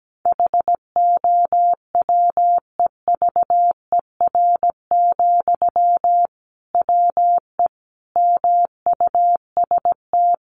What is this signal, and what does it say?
Morse code, heard as a keyed tone, about 17 wpm, 700 Hz. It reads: HOWEVER, WE MUST
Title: Sentences from Top 600 Words 17wpm